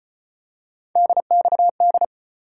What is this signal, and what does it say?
Morse code, heard as a keyed tone, about 33 wpm, 700 Hz. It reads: DXD